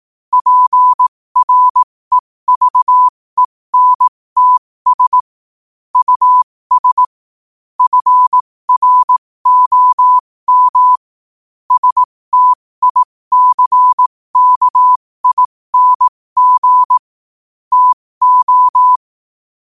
Morse code, heard as a keyed tone, about 18 wpm, spaced as for 15 wpm, 1000 Hz. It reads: PREVENTS US FROM STICKING TO
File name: qotdfiles / 15wpm